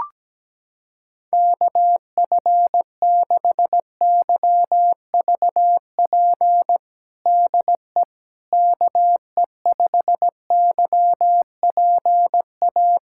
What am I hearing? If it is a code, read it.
Morse code, heard as a keyed tone, about 17 wpm, 700 Hz. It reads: KF6YVP DE KE5YPA
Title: USA Callsign DE Callsign 17wpm